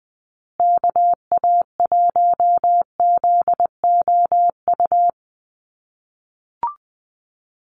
Morse code, heard as a keyed tone, about 20 wpm, 700 Hz. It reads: KA1ZOU
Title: USA Call Signs Repeated 2x 20wpm